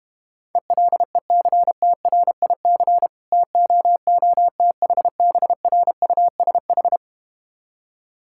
Morse code, heard as a keyed tone, about 31 wpm, 700 Hz. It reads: ELECTRIC TOOTHBRUSH